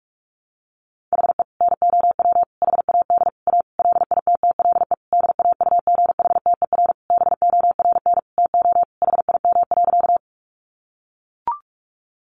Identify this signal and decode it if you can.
Morse code, heard as a keyed tone, about 44 wpm, 700 Hz. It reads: HE NOW HAD A LITTLE DAUGHTER BORN TO HIM.
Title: Sentences from Top 1500 Words 45wpm